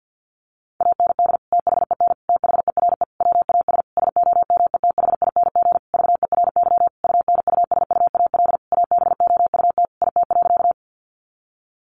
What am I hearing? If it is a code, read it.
Morse code, heard as a keyed tone, about 50 wpm, 700 Hz. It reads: AND THEN THERE WAS SOMETHING VERY UNUSUAL ABOUT IT.
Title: Sentences from Top 1400 Words 50wpm